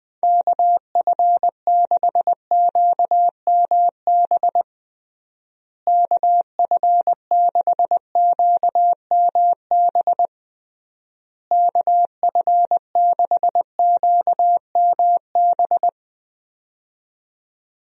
Morse code, heard as a keyed tone, about 20 wpm, 700 Hz. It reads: KF6QMB KF6QMB KF6QMB
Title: USA Call Signs - 6 Characters - Repeated 3x - 20wpm